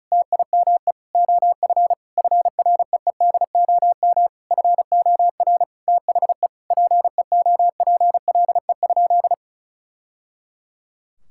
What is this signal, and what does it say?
Morse code, heard as a keyed tone, about 36 wpm, 700 Hz. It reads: TIME OF FREEDOM FOR THE PEOPLE?
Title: Sentences from Top 1000 Words - Review Part 5 - 35wpm